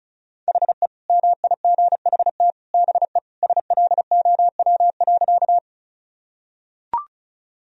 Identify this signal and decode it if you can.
Morse code, heard as a keyed tone, about 34 wpm, 700 Hz. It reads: HE MIGHT BE SLOW.
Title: Sentences from Top 1000 Words - Review Part 2 - 35wpm